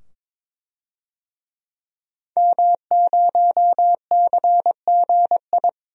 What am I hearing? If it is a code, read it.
Morse code, heard as a keyed tone, about 22 wpm, 700 Hz. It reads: M0CGI
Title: DXCC-100 Call Signs - 5 Characters - 22wpm